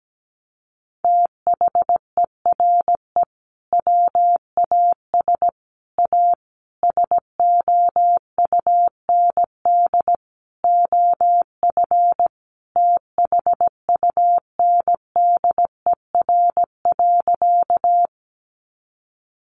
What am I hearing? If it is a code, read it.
Morse code, heard as a keyed tone, about 17 wpm, 700 Hz. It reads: THERE WAS A SOUND OF THUNDER.
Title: A Sound Of Thunder 17wpm